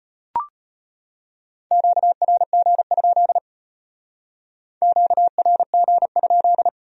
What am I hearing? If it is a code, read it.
Morse code, heard as a keyed tone, about 32 wpm, 700 Hz. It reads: QRG? QRG?